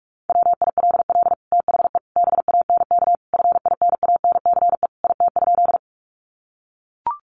Morse code, heard as a keyed tone, about 44 wpm, 700 Hz. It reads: WILL THE BANK FINANCE IT?
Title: Sentences from Top 1500 Words 45wpm